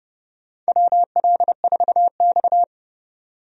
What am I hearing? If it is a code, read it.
Morse code, heard as a keyed tone, about 30 wpm, 700 Hz. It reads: WL4X
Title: USA Call Signs - 4 Characters - 30wpm